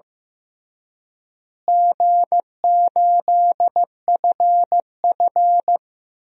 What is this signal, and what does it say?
Morse code, heard as a keyed tone, about 16 wpm, 700 Hz. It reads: G8FF
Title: DXCC-100 Call Signs - 4 Characters - Repeated 2x - 15wpm